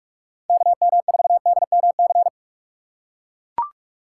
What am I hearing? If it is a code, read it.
Morse code, heard as a keyed tone, about 43 wpm, 700 Hz. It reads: KM4DMC